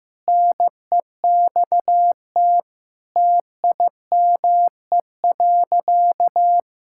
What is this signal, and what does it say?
Morse code, heard as a keyed tone, about 14 wpm, 700 Hz. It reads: NEXT TIME.
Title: Sentences from Top 1600 Words 15wpm